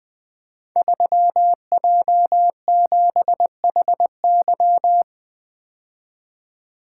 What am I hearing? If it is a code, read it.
Morse code, heard as a keyed tone, about 20 wpm, 700 Hz. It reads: 3J7HY
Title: Code Groups - 5 Letter:Numbers - 20wpm